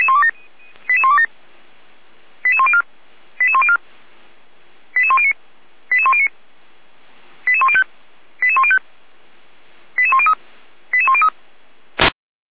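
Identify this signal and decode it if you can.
Five-tone selective call (ZVEI), heard as quick runs of five tones, tones of about 70 ms. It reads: 88127 88104 88104 88109 88109 88105 88105 88103 88103
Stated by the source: Five-tone ZVEI selective call 88164 received on an SDR.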